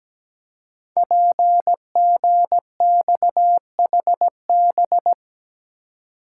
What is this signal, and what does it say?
Morse code, heard as a keyed tone, about 17 wpm, 700 Hz. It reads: PGXHB
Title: Code Groups - 5 Letters - 17wpm